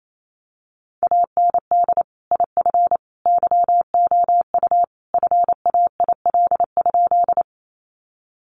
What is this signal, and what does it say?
Morse code, heard as a keyed tone, about 28 wpm, 700 Hz. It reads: AND IF YOU FAIL?
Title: Sentences from Top 1100 Words 28wpm